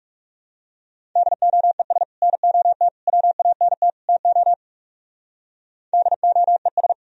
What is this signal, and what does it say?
Morse code, heard as a keyed tone, about 36 wpm, 700 Hz. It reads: DOES NOT WANT TO DOES
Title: Speed-Racing - Sets of 4 Words 30wpm